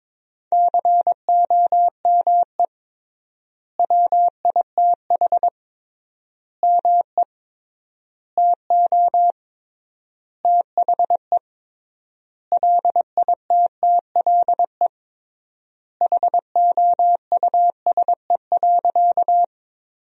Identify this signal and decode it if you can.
Morse code, heard as a keyed tone, about 22 wpm, 700 Hz. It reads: COME WITH ME TO THE LITTLE HOUSE.
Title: 3x Word-Spacing - Sentences from Top 200 Words - 22wpm